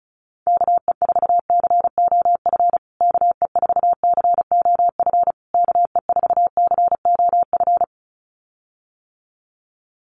Morse code, heard as a keyed tone, about 34 wpm, 700 Hz. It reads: KE4COF KE4COF KE4COF
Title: USA Call Signs Repeated 3x 35wpm